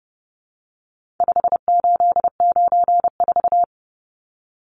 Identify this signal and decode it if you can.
Morse code, heard as a keyed tone, about 30 wpm, 700 Hz. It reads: 5894
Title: Numbers - 4-Digit 30wpm